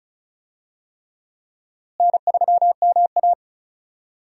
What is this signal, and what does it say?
Morse code, heard as a keyed tone, about 34 wpm, 700 Hz. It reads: N3MA
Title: USA Call Signs - 4 Characters - Repeated 3x - 35wpm